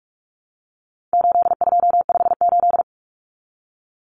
Morse code, heard as a keyed tone, about 44 wpm, 700 Hz. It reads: 8258
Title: Common Cut Numbers - 4-Digit 45wpm